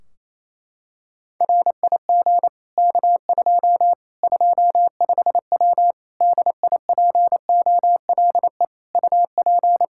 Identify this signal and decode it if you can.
Morse code, heard as a keyed tone, about 28 wpm, 700 Hz. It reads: RIG K2 25W DIPOLE UP
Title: Practice QSO - Exchange 2 - Info - 28wpm